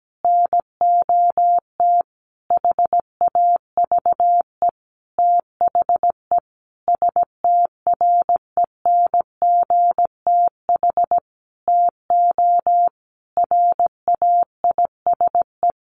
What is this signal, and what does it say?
Morse code, heard as a keyed tone, about 17 wpm, 700 Hz. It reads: NOT HAVE THE STRENGTH TO RAISE
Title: Sentences from Top 1400 Words 17wpm